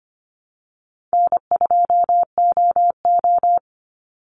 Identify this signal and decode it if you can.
Morse code, heard as a keyed tone, about 25 wpm, 700 Hz. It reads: N2OO
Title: Call Signs Repeated 3x - 25wpm